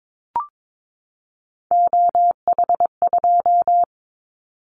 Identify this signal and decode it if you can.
Morse code, heard as a keyed tone, about 22 wpm, 700 Hz. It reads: OH2
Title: Callsign Prefixes for Top 25 DXCC - 22wpm